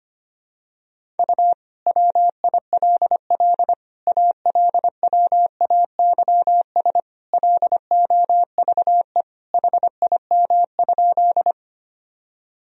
Morse code, heard as a keyed tone, about 25 wpm, 700 Hz. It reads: U WILL ALWAYS LOVE HIM?
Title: Sentences from Top 400 Words and Abbreviations 25wpm